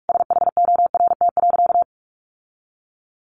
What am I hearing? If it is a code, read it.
Morse code, heard as a keyed tone, about 44 wpm, 700 Hz. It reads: SHORT.
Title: Sentences from Top 1400 Words 45wpm